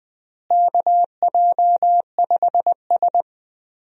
Morse code, heard as a keyed tone, about 21 wpm, 700 Hz. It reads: KJ5S
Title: USA Call Signs - 4 Characters - Repeated 2x - 20wpm